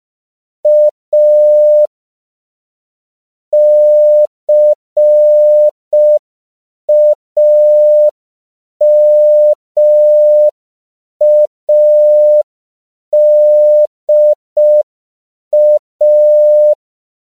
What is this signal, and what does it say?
Morse code, heard as a keyed tone, about 5 wpm, 600 Hz. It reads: A CAMADA
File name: CW / 5PPM